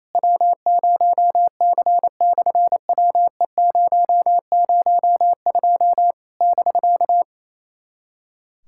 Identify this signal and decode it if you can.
Morse code, heard as a keyed tone, about 28 wpm, 700 Hz. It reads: W0C/WE002 <BK>
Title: SOTA QSO Practice 28wpm